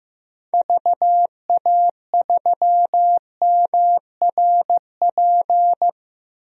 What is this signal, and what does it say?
Morse code, heard as a keyed tone, about 15 wpm, 700 Hz. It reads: VA3MRP
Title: DXCC-100 Call Signs - 6 Characters - Repeated 2x - 15wpm